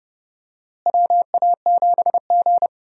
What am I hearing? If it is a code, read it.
Morse code, heard as a keyed tone, about 30 wpm, 700 Hz. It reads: WA7G
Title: USA Call Signs - 4 Characters - Repeated 2x - 30wpm